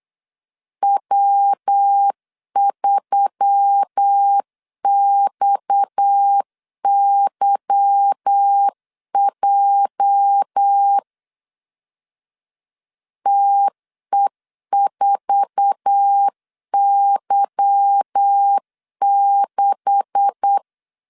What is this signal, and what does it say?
Morse code, heard as a keyed tone, about 8 wpm, 800 Hz. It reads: W3XYJ TE4Y6
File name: LEZIONI / 9 286 8wpm 3425